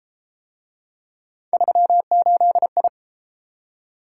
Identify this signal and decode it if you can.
Morse code, heard as a keyed tone, about 33 wpm, 700 Hz. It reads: 38I